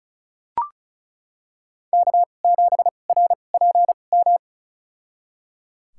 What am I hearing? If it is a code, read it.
Morse code, heard as a keyed tone, about 34 wpm, spaced as for 25 wpm, 690 Hz. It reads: K7RPM
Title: USA Call Signs - 5 Characters - 2x Character Spacing - 35wpm